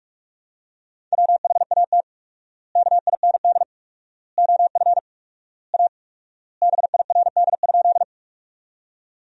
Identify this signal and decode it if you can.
Morse code, heard as a keyed tone, about 46 wpm, 700 Hz. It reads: WHAT KIND OF A BIRD?
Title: 4x Word-Spacing - Sentences from Top 400 Words - 45wpm